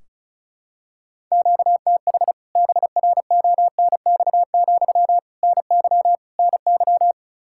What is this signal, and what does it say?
Morse code, heard as a keyed tone, about 34 wpm, 700 Hz. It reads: QTH BRONX, NY NY
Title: K1USN SST QSO Practice 35wpm